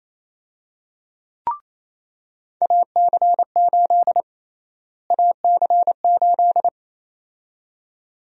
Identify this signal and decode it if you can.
Morse code, heard as a keyed tone, about 27 wpm, 700 Hz. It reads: AC8 AC8
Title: USA Call Sign Prefixes - Repeated 2x - 28wpm